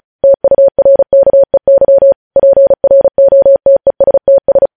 Morse code, heard as a keyed tone, about 36 wpm, 550 Hz. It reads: TURKEY PROTESTS
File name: podcast / News headlines-1x-35x35